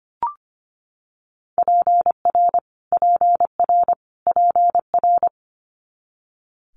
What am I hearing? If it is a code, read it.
Morse code, heard as a keyed tone, about 25 wpm, 700 Hz. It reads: PR PR PR
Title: Multi-letter Phonemes Repeated 3x 25wpm